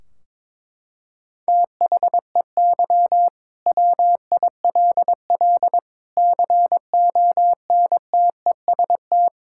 Morse code, heard as a keyed tone, about 22 wpm, 700 Hz. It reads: THEY WILL CONTEST
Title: Sentences from Top 1600 Words 22wpm